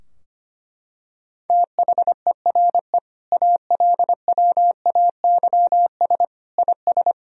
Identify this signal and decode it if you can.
Morse code, heard as a keyed tone, about 25 wpm, 700 Hz. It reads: THERE ALWAYS IS
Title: Sentences from Top 1000 Words - Review Part 2 - 25wpm